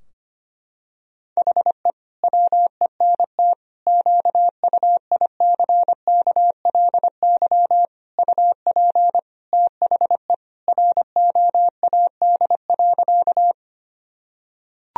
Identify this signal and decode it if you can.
Morse code, heard as a keyed tone, about 25 wpm, 700 Hz. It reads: HE WENT QUICKLY UP THE ROAD.
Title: Sentences from Top 1000 Words - Review Part 2 - 25wpm